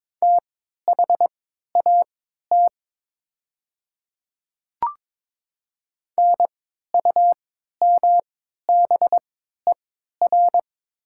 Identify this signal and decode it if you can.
Morse code, heard as a keyed tone, about 22 wpm, spaced as for 12 wpm, 700 Hz. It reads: THAT NUMBER
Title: 3x Character Spacing - Top 100 Words - 22wpm